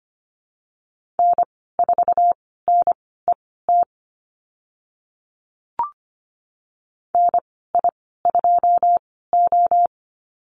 Morse code, heard as a keyed tone, about 25 wpm, spaced as for 16 wpm, 700 Hz. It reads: N4NET NI2O
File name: Lesson 15g - Number 5 - Call Signs 25wpm